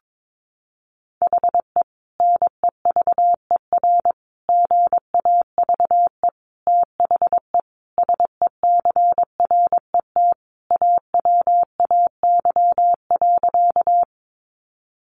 Morse code, heard as a keyed tone, about 22 wpm, 700 Hz. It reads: HE NEVER GAVE THE SECRET AWAY.
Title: Sentences from Top 1200 Words 22wpm